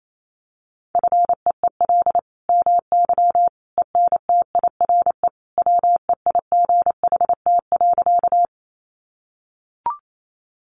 Morse code, heard as a keyed tone, about 28 wpm, 700 Hz. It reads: FEEL MY ENTIRE WEIGHT.
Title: Sentences from Top 800 Words 28wpm